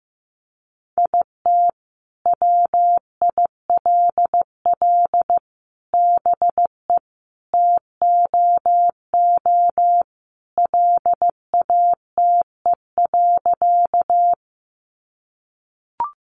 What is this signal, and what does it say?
Morse code, heard as a keyed tone, about 15 wpm, 700 Hz. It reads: IT WILL BE TOO LATE.
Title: Sentences from Top 300 Words 15wpm